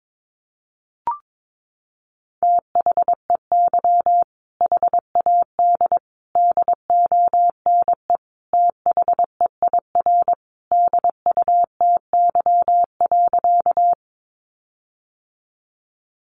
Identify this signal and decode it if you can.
Morse code, heard as a keyed tone, about 22 wpm, 700 Hz. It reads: THEY HAD DONE THEIR DUTY.
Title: Sentences from Top 1500 Words 22wpm